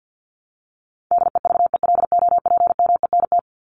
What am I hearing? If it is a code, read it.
Morse code, heard as a keyed tone, about 50 wpm, 700 Hz. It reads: DEVELOPMENT
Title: Top 501-750 Words 50wpm